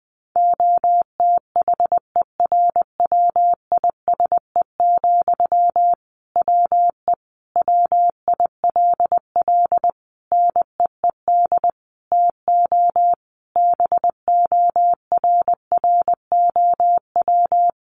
Morse code, heard as a keyed tone, about 20 wpm, 700 Hz. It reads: OTHERWISE, WE WILL NEED TO BORROW
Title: Sentences from Top 1500 Words 20wpm